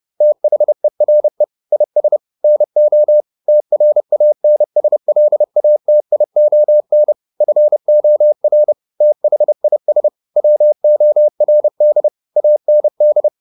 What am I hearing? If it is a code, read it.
Morse code, heard as a keyed tone, about 30 wpm, 600 Hz. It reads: THERE IS NO TRANSLATION FOR THIS WORD AND